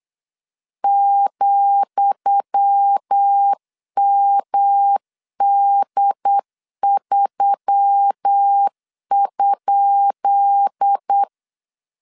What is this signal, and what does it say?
Morse code, heard as a keyed tone, about 8 wpm, 800 Hz. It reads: ,MD3?
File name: LEZIONI / 10 SIGNS / 10D COMA 8wpm 3749